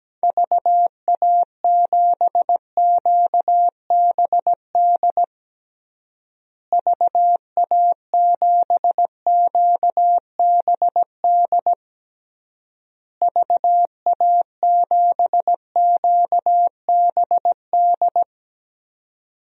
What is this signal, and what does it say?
Morse code, heard as a keyed tone, about 17 wpm, 700 Hz. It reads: VA7QBD VA7QBD VA7QBD
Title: DXCC-100 Call Signs - 6 Characters - Repeated 3x - 17wpm